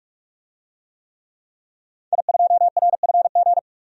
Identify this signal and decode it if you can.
Morse code, heard as a keyed tone, about 44 wpm, 700 Hz. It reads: I2RFG
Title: DXCC-100 Call Signs - 5 Characters - 45wpm